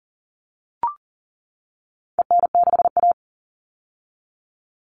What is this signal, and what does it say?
Morse code, heard as a keyed tone, about 40 wpm, 700 Hz. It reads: EN6A
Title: All Cut Numbers - 4-Digit 40wpm